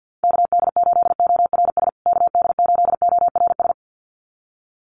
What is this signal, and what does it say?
Morse code, heard as a keyed tone, about 50 wpm, 700 Hz. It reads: KD8ORS KD8ORS
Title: USA Call Signs Repeated 2x 50wpm